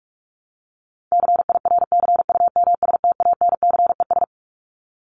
Morse code, heard as a keyed tone, about 44 wpm, 700 Hz. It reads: CIRCUMSTANCES
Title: ES:S Suffix - Top 100 Words with Base Words 45wpm